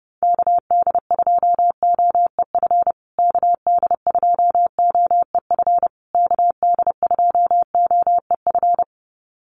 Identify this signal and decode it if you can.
Morse code, heard as a keyed tone, about 30 wpm, 700 Hz. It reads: KD2OEF KD2OEF KD2OEF
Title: USA Call Signs Repeated 3x 30wpm